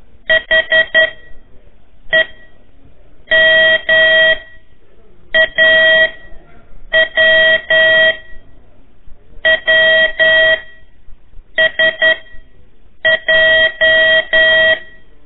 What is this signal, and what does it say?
Morse code, heard as a keyed tone, about 11 wpm, 1900 Hz. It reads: H E M A W W S J
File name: 3 lezione CW 1 AIR